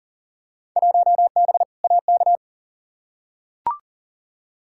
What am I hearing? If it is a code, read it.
Morse code, heard as a keyed tone, about 37 wpm, 700 Hz. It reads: 1B AK
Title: Field Day Exchanges Encore - 40wpm